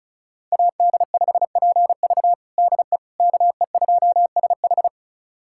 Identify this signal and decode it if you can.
Morse code, heard as a keyed tone, about 36 wpm, 700 Hz. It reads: AD5PV DE KE2SH
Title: USA Callsign DE Callsign - 5 Characters - 35wpm